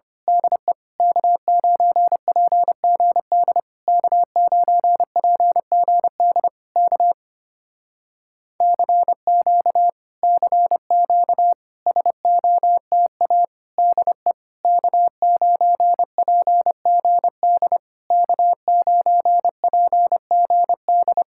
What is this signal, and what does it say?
Morse code, heard as a keyed tone, about 24 wpm, 700 Hz. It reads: DE K9PGD K9PGD K CQ CQ SOTA DE K9PGD K9PGD
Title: Speed-Racing Common QSO Phrases Encore - 25wpm